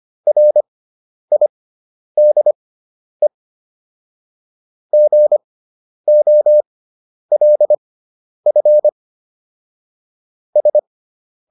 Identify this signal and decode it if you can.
Morse code, heard as a keyed tone, about 25 wpm, spaced as for 10 wpm, 600 Hz. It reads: RIDE GOLF S